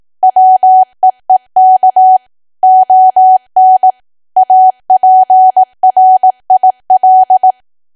Morse code, heard as a keyed tone, about 18 wpm, 750 Hz. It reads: WEEK ON APRIL
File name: cwa / W1AW-Level2 / 18WPM